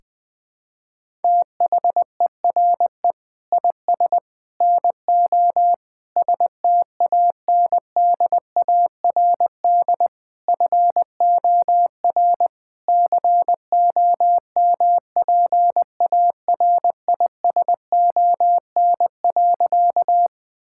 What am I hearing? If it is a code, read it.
Morse code, heard as a keyed tone, about 21 wpm, 700 Hz. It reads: THERE IS NO STANDARD FOR COMPARISON.
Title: Sentences from Top 1600 Words 20wpm